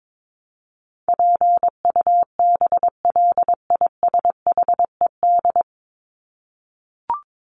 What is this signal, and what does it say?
Morse code, heard as a keyed tone, about 22 wpm, 700 Hz. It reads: PUBLISHED
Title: ED Suffix - Top 100 Words with Base Words 22wpm